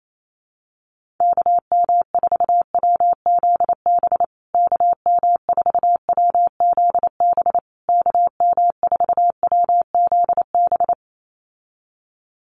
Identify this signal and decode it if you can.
Morse code, heard as a keyed tone, about 28 wpm, 700 Hz. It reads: KM4WZB KM4WZB KM4WZB
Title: USA Call Signs Repeated 3x 28wpm